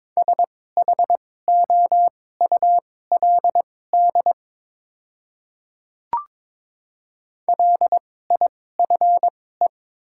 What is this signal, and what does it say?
Morse code, heard as a keyed tone, about 22 wpm, spaced as for 16 wpm, 700 Hz. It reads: SHOULD LIFE
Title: 2x Character Spacing - Top 300 Words - 22wpm